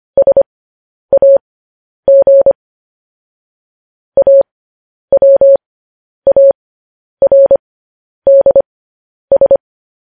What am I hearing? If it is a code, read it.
Morse code, heard as a keyed tone, about 25 wpm, spaced as for 10 wpm, 550 Hz. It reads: SAG AWARDS